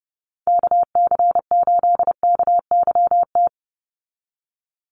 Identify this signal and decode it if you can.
Morse code, heard as a keyed tone, about 30 wpm, 700 Hz. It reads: KC8KYT
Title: USA Call Signs 30wpm